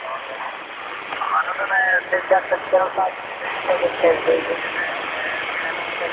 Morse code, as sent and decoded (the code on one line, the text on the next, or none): none